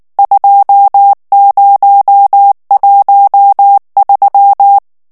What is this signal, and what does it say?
Morse code, heard as a keyed tone, about 19 wpm, 800 Hz. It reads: 2013
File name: ANGLAIS texte clair 3 vitesse 18 mots minute